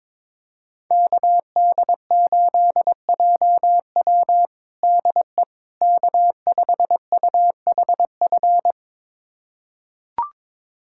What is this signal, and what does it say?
Morse code, heard as a keyed tone, about 21 wpm, 700 Hz. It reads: KD8JW DE K5UHF
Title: USA Callsign DE Callsign - 5 Characters - 22wpm